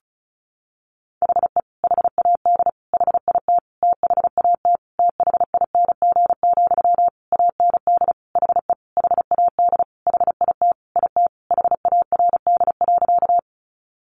Morse code, heard as a keyed tone, about 34 wpm, 700 Hz. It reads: HE HAD HIT THAT THING, AND HE HAD HIT IT HARD.
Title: Sentences from Top 1200 Words 35wpm